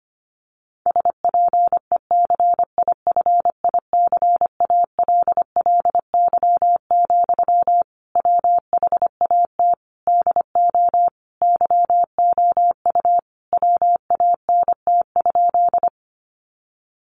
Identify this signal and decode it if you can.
Morse code, heard as a keyed tone, about 25 wpm, 700 Hz. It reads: SPECIFICALLY, WHAT DO YOU WANT?
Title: Sentences from Top 1100 Words 25wpm